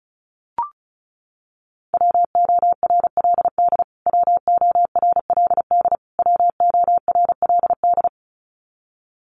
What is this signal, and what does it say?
Morse code, heard as a keyed tone, about 34 wpm, 700 Hz. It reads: WORLD WORLD WORLD
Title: Top 200 Words Repeated 35wpm